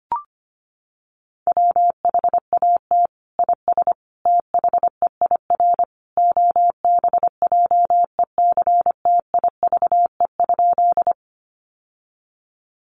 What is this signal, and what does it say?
Morse code, heard as a keyed tone, about 25 wpm, 700 Hz. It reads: WHAT IS THEIR OBJECTIVE?
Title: Sentences from Top 1200 Words 25wpm